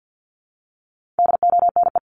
Morse code, heard as a keyed tone, about 48 wpm, 700 Hz. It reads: DONE
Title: Top 500 Words Repeated 50wpm